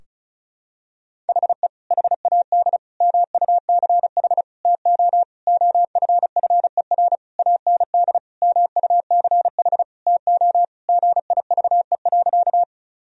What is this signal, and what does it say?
Morse code, heard as a keyed tone, about 34 wpm, 700 Hz. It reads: HE HAD MUCH TO OFFER AND MUCH TO GIVE.
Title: Sentences from Top 1000 Words - Review Part 4 - 35wpm